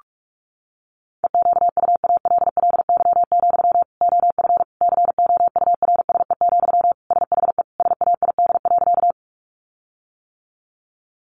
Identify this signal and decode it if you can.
Morse code, heard as a keyed tone, about 44 wpm, 700 Hz. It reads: EQUALLY, OF COURSE, SHE SAID.
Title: Sentences from Top 1400 Words 45wpm